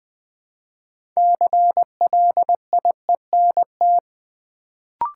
Morse code, heard as a keyed tone, about 20 wpm, 700 Hz. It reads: CLIENT
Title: Build A Word - 3 Letters At A Time - 6-Letter Words - 20wpm